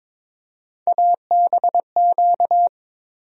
Morse code, heard as a keyed tone, about 22 wpm, 700 Hz. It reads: ABQ
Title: USA Call Sign Postfixes - 3 Letters 22wpm